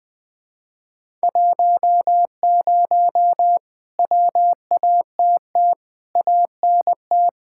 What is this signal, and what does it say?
Morse code, heard as a keyed tone, about 20 wpm, 700 Hz. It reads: 10 WATT ANT
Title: Speed-Racing Common QSO Phrases Encore - 20wpm